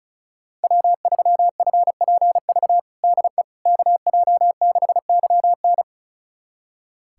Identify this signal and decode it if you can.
Morse code, heard as a keyed tone, about 36 wpm, 700 Hz. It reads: W3FPV DE KJ6YN
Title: USA Callsign DE Callsign - 5 Characters - 35wpm